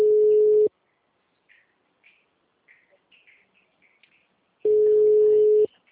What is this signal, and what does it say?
Ringback tone (Europe, 425 Hz), with speech.
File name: out- -2043- - - .36350